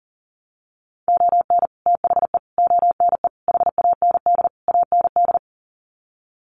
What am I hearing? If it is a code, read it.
Morse code, heard as a keyed tone, about 40 wpm, 700 Hz. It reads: ON THE ONE HAND AND
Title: Sets of 5 Words Encore2 40wpm